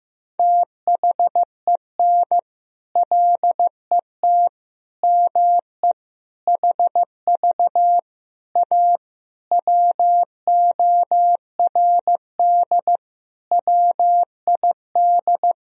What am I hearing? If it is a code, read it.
Morse code, heard as a keyed tone, about 15 wpm, 700 Hz. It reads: THEN LET ME HV A WORD WID
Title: Sentences from Top 300 Words and Abbreviations 15wpm